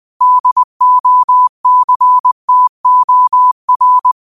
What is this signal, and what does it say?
Morse code, heard as a keyed tone, about 20 wpm, 1000 Hz. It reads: DOCTOR
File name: qotdfiles / 20wpm